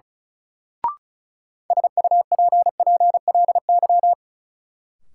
Morse code, heard as a keyed tone, about 36 wpm, 700 Hz. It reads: SUPPLY
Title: Sentences from Top 1200 Words - New Word Drill Alt - 35wpm